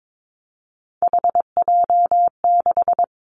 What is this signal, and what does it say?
Morse code, heard as a keyed tone, about 22 wpm, 700 Hz. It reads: HJ6
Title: Callsign Prefixes for Top 100 DXCC - 22wpm